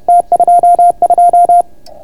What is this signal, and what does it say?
Morse code, heard as a keyed tone, about 31 wpm, 690 Hz. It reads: T22